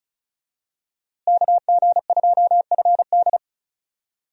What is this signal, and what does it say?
Morse code, heard as a keyed tone, about 36 wpm, 700 Hz. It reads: KG2FD
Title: USA Call Signs - 5 Characters - Repeated 2x - 35wpm